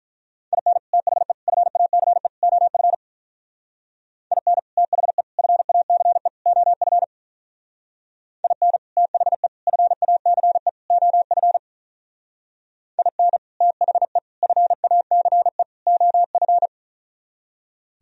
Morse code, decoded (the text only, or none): IN THE FACE OF IN THE FACE OF IN THE FACE OF IN THE FACE OF